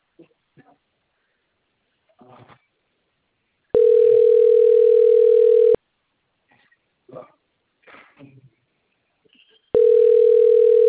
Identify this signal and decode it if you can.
Ringback tone (North America), 440 + 480 Hz.